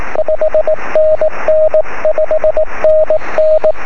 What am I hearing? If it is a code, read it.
Morse code, heard as a keyed tone, about 19 wpm, 620 Hz. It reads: 5NN 5NN
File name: sm7lcb / 23-sk7mw